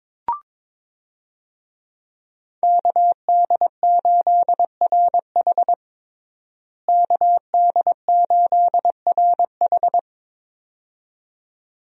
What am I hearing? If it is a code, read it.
Morse code, heard as a keyed tone, about 22 wpm, 700 Hz. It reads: KD8RH KD8RH
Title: USA Call Signs - 5 Characters - Repeated 2x - 22wpm